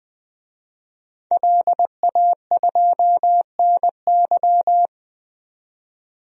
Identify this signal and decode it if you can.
Morse code, heard as a keyed tone, about 21 wpm, 700 Hz. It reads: LA2NY
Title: DXCC-100 Call Signs - 5 Characters - 20wpm